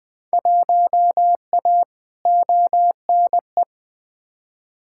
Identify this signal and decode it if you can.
Morse code, heard as a keyed tone, about 20 wpm, 700 Hz. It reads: 1A ONE
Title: Field Day Exchanges Encore - 20wpm